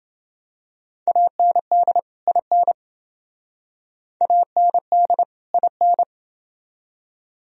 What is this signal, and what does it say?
Morse code, heard as a keyed tone, about 28 wpm, 700 Hz. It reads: AND IN AND IN